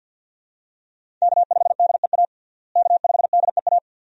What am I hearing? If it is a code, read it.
Morse code, heard as a keyed tone, about 49 wpm, 700 Hz. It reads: K5DEA K5DEA
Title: USA Call Signs - 5 Characters - Repeated 2x - 50wpm